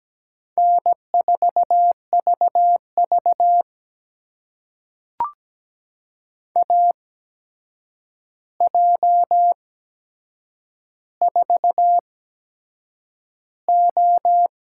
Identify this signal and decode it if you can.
Morse code, heard as a keyed tone, about 17 wpm, spaced as for 7 wpm, 700 Hz. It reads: N4VV AJ4O
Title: USA Call Signs - 8x Character Spacing - 17wpm